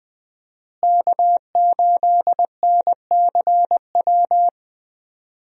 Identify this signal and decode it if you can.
Morse code, heard as a keyed tone, about 20 wpm, 700 Hz. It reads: K8NCW
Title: USA Call Signs - 5 Characters - Repeated 2x - 20wpm